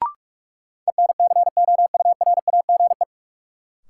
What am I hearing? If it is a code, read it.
Morse code, heard as a keyed tone, about 42 wpm, 700 Hz. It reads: ENCOURAGE
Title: Sentences from Top 1400 Words - New Word Drill Alt - 45wpm